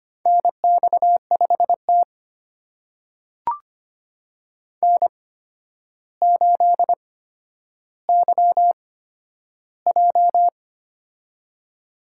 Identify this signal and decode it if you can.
Morse code, heard as a keyed tone, about 25 wpm, spaced as for 11 wpm, 700 Hz. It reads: NX5T N8YJ
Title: USA Call Signs - 4 Characters - 8x Character Spacing - 25wpm